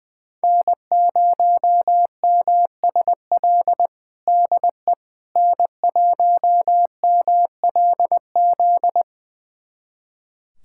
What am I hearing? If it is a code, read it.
Morse code, heard as a keyed tone, about 21 wpm, 700 Hz. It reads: N0MSL DE N1MLZ
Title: USA Callsign DE Callsign - 5 Characters - 20wpm